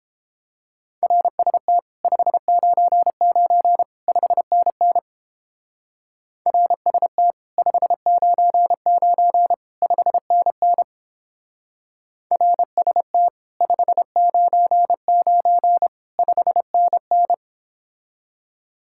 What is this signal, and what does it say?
Morse code, heard as a keyed tone, about 29 wpm, 700 Hz. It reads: RST 599 5NN RST 599 5NN RST 599 5NN